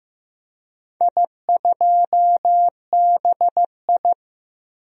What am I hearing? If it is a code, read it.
Morse code, heard as a keyed tone, about 15 wpm, 700 Hz. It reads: I2BI
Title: DXCC-100 Call Signs - 4 Characters - Repeated 2x - 15wpm